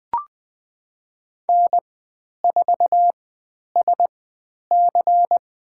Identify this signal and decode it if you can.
Morse code, heard as a keyed tone, about 20 wpm, spaced as for 10 wpm, 700 Hz. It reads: N4SC
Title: 102 Lesson 17g - Letter u - Call Signs 20wpm